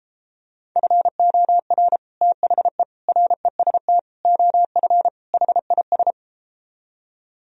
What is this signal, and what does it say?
Morse code, heard as a keyed tone, about 33 wpm, 700 Hz. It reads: FOR THE REST OF HIS